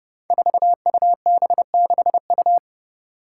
Morse code, heard as a keyed tone, about 30 wpm, 700 Hz. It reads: 4UB6U
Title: Five Challenging Letter-Number Combinations 30wpm